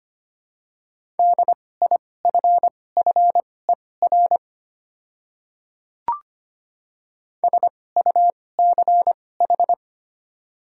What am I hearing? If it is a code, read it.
Morse code, heard as a keyed tone, about 25 wpm, spaced as for 18 wpm, 690 Hz. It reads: DIFFER SUCH
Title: 2x Character Spacing - Top 200 Words - 25wpm